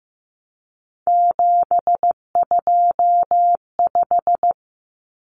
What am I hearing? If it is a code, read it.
Morse code, heard as a keyed tone, about 15 wpm, 700 Hz. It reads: 725
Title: Common Cut Numbers - 3-Digit 15wpm